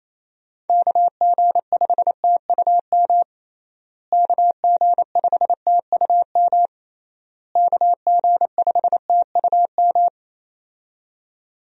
Morse code, heard as a keyed tone, about 28 wpm, 700 Hz. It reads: KG5TUM KG5TUM KG5TUM
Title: USA Call Signs - 6 Characters - Repeated 3x - 28wpm